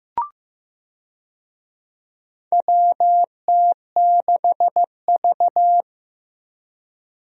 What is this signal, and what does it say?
Morse code, heard as a keyed tone, about 15 wpm, 700 Hz. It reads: WT6V
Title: USA Call Signs - 4 Characters - Repeated 3x - 15wpm